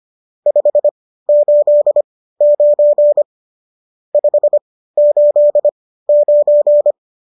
Morse code, heard as a keyed tone, about 25 wpm, spaced as for 15 wpm, 590 Hz. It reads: 589 589
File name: qso204 15